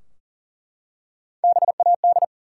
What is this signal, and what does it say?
Morse code, heard as a keyed tone, about 41 wpm, 700 Hz. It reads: BAD
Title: Three Letter Words - 40wpm